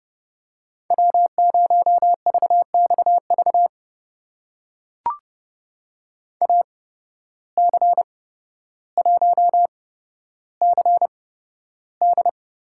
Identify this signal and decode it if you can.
Morse code, heard as a keyed tone, about 31 wpm, spaced as for 13 wpm, 700 Hz. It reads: W0VXV AC1CD
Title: USA Call Signs - 5 Characters - 8x Character Spacing - 30wpm